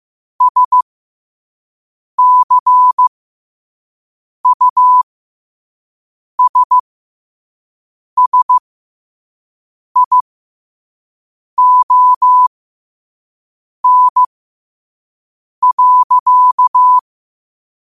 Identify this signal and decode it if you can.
Morse code, heard as a keyed tone, about 15 wpm, spaced as for 5 wpm, 1000 Hz. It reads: SCUSSION.
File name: qotdfiles / 5wpm